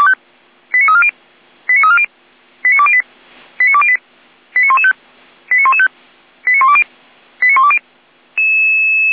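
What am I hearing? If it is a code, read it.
Five-tone selective call (ZVEI), heard as quick runs of five tones, tones of about 70 ms. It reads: 89340 89340 89208 89208 89115 89115 89120 89120
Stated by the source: ZVEI five-tone selective call received on an SDR.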